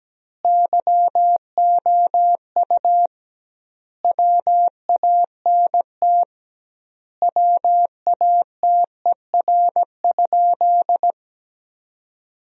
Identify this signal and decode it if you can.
Morse code, heard as a keyed tone, about 17 wpm, 700 Hz. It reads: YOU WANT WATER?
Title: 2x Word-Spacing - Sentences from Top 200 Words - 17wpm